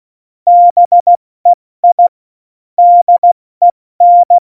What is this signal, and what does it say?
Morse code, heard as a keyed tone, about 16 wpm, spaced as for 14 wpm, 700 Hz. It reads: BEI DEN
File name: Morsekurs / Witze / Witz 21 Juli 2020, 70 F80